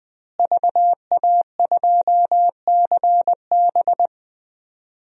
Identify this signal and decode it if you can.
Morse code, heard as a keyed tone, about 20 wpm, 700 Hz. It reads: VA2CB
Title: DXCC-100 Call Signs - 5 Characters - Repeated 2x - 20wpm